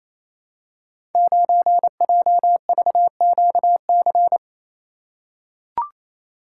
Morse code, heard as a keyed tone, about 28 wpm, 700 Hz. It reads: 9JVQC
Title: Code Groups - 5 Letter:Numbers - 28wpm